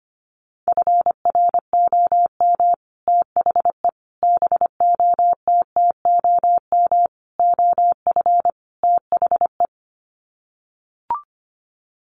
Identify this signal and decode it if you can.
Morse code, heard as a keyed tone, about 25 wpm, 700 Hz. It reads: FROM THE BOTTOM OF THE
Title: Sets of 5 Words Encore2 25wpm